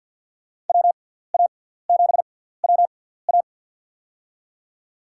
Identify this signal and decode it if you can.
Morse code, heard as a keyed tone, about 52 wpm, spaced as for 17 wpm, 700 Hz. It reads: WA7WA
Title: USA Call Signs - 5 Characters - 6x Character Spacing - 50wpm